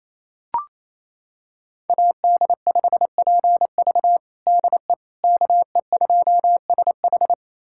Morse code, heard as a keyed tone, about 28 wpm, 700 Hz. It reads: AD5PV DE KE2SH
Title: USA Callsign DE Callsign - 5 Characters - 28wpm